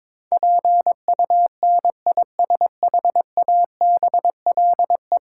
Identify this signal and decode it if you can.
Morse code, heard as a keyed tone, about 22 wpm, 700 Hz. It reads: PUNISHABLE
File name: Sentences with 7-Word Phrases - 22wpm